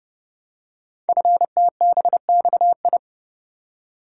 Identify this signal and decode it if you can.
Morse code, heard as a keyed tone, about 30 wpm, 700 Hz. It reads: FTBXI
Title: Code Groups - 5 Letters - 30wpm